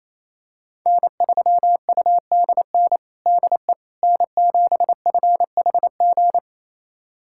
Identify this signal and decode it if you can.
Morse code, heard as a keyed tone, about 28 wpm, 700 Hz. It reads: N3UDN DE N7FHG
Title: USA Callsign DE Callsign - 5 Characters - 28wpm